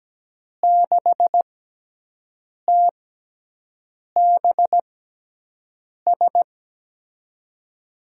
Morse code, heard as a keyed tone, about 17 wpm, spaced as for 6 wpm, 700 Hz. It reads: 6TBS
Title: USA Call Signs - 6 Characters - 6x Character Spacing - 17wpm